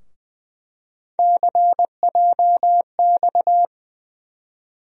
Morse code, heard as a keyed tone, about 21 wpm, 700 Hz. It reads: CJX
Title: USA Call Sign Postfixes - 3 Letters 20wpm